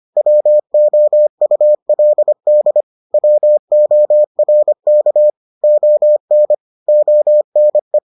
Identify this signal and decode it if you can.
Morse code, heard as a keyed tone, about 25 wpm, 600 Hz. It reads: WOULD WORK ON ONE